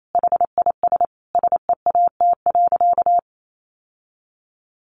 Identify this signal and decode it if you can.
Morse code, heard as a keyed tone, about 28 wpm, 700 Hz. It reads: HIS SEAT.
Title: Sentences from Top 1500 Words 28wpm